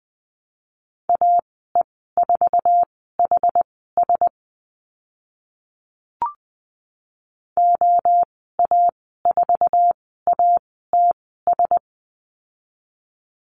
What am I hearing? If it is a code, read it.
Morse code, heard as a keyed tone, about 20 wpm, spaced as for 15 wpm, 700 Hz. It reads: AE4HS OA4ATS
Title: 062 Lesson 11g - Letter h - Call Signs 20wpm